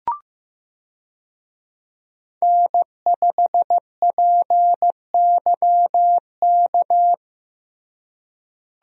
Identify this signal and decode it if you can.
Morse code, heard as a keyed tone, about 15 wpm, 700 Hz. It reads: N5PYK
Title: USA Call Signs - 5 Characters - Repeated 2x - 15wpm